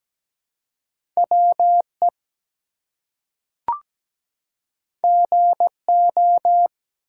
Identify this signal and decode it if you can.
Morse code, heard as a keyed tone, about 17 wpm, 700 Hz. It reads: WE GO
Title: Two Letter Words - 17wpm